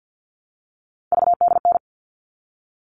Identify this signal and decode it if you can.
Morse code, heard as a keyed tone, about 50 wpm, 700 Hz. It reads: VDN
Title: All Cut Numbers - 3-Digit 50wpm